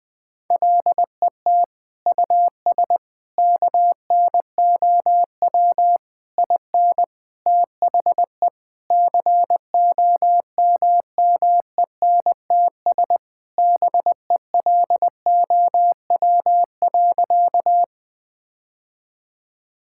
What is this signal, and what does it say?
Morse code, heard as a keyed tone, about 20 wpm, 700 Hz. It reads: LET US KNOW IN THE COMMENTS BELOW.
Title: Sentences with 7-Word Phrases - 20wpm